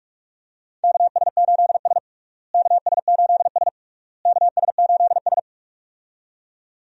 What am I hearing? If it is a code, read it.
Morse code, heard as a keyed tone, about 41 wpm, 700 Hz. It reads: KS8S KS8S KS8S
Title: USA Call Signs - 4 Characters - Repeated 3x - 45wpm